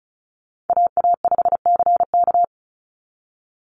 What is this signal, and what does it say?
Morse code, heard as a keyed tone, about 36 wpm, 700 Hz. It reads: AA5CK
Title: NA SOTA Chaser Callsigns 35wpm